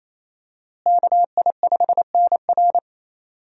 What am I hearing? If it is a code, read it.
Morse code, heard as a keyed tone, about 28 wpm, 700 Hz. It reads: KI5NR
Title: USA Call Signs - 5 Characters - Repeated 2x - 28wpm